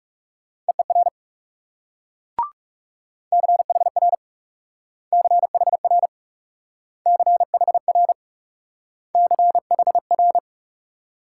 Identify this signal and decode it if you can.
Morse code, heard as a keyed tone, about 40 wpm, 700 Hz. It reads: EER CHR CHR CHR CHR